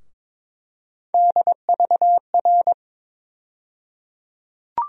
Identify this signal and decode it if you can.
Morse code, heard as a keyed tone, about 22 wpm, 700 Hz. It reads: DVR
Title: USA Call Sign Postfixes - 3 Letters 22wpm